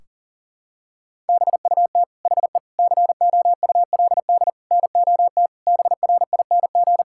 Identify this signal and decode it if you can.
Morse code, heard as a keyed tone, about 40 wpm, 700 Hz. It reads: BUT HE COULD NOT BRING
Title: Sentences from Top 1600 Words 40wpm